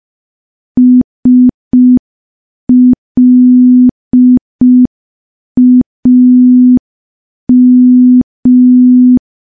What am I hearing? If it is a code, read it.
Morse code, heard as a keyed tone, about 5 wpm, 250 Hz. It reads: SLAM
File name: morse / morse-training-05WPM